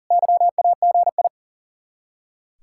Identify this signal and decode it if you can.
Morse code, heard as a keyed tone, about 53 wpm, 700 Hz. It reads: YAGI